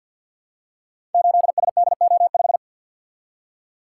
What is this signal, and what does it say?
Morse code, heard as a keyed tone, about 51 wpm, 700 Hz. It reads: 8SDO5
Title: Code Groups - 5 Letter:Numbers - 50wpm